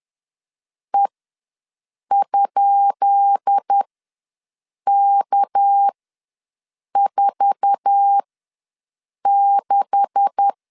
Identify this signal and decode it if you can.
Morse code, heard as a keyed tone, about 10 wpm, spaced as for 6 wpm, 800 Hz. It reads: E?K46
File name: LEZIONI / 10 SIGNS / 10B INT ws 10wpm 4134